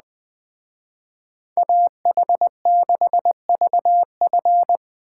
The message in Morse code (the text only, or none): AH6VF